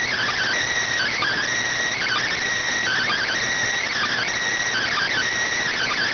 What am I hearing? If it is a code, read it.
Emergency Alert System (SAME) tones (header bursts), heard as harsh buzzing bursts. It is heard from 0.9 to 5.3 s.